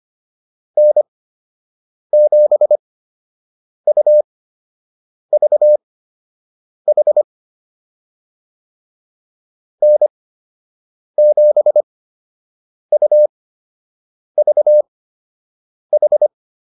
Morse code, heard as a keyed tone, about 25 wpm, spaced as for 7 wpm, 600 Hz. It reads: N7UVH N7UVH